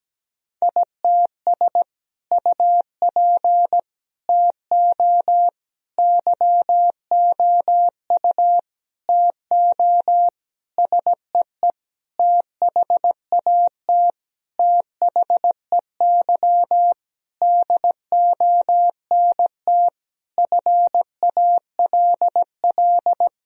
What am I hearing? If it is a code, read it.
Morse code, heard as a keyed tone, about 17 wpm, 700 Hz. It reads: ITS UP TO YOU TO SEE THAT THEY DONT FALL
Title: Sentences from Top 1000 Words - Review Part 2 - 17wpm